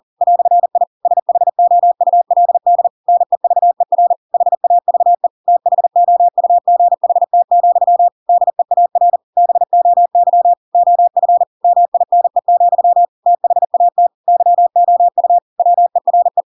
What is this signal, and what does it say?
Morse code, heard as a keyed tone, about 40 wpm, 700 Hz. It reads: "I SHOULD NEVER HAVE THOUGHT, DEAR BOY OF MINE, THAT YOU WERE
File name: Pinocchio / Pinocchio ch7 40 40 ik1hge